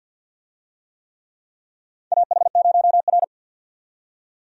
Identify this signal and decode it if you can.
Morse code, heard as a keyed tone, about 50 wpm, 700 Hz. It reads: AH0R
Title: USA Call Signs - 4 Characters - 50wpm